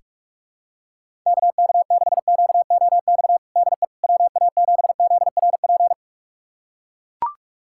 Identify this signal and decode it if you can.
Morse code, heard as a keyed tone, about 42 wpm, 700 Hz. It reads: KK6QOX DE WA7ZRP
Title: USA Callsign DE Callsign - 6 Characters - 45wpm